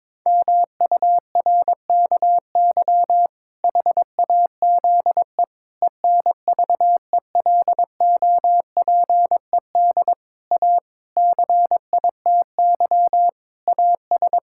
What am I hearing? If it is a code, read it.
Morse code, heard as a keyed tone, about 22 wpm, 700 Hz. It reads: MURKY HAZE ENVELOPED A CITY AS
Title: Pangram Challenge v2 - 22wpm